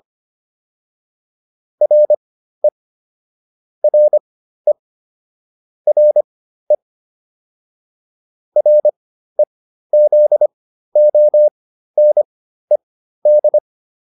Morse code, heard as a keyed tone, about 25 wpm, spaced as for 13 wpm, 600 Hz. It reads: RE RE RE REZONED